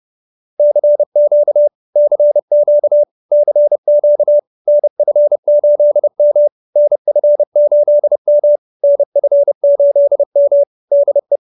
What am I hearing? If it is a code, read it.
Morse code, heard as a keyed tone, about 30 wpm, 590 Hz. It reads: CQ CQ CQ NF8M NF8M NF8M DE
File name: QSO321 30